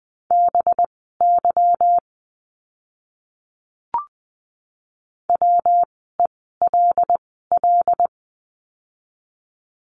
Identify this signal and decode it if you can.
Morse code, heard as a keyed tone, about 20 wpm, spaced as for 15 wpm, 700 Hz. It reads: BY WELL
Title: 206 Lesson 33f - Letter j - Words 20wpm